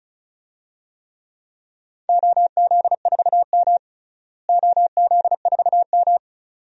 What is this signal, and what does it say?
Morse code, heard as a keyed tone, about 36 wpm, 700 Hz. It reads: OZ4M OZ4M
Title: DXCC-100 Call Signs - 4 Characters - Repeated 2x - 35wpm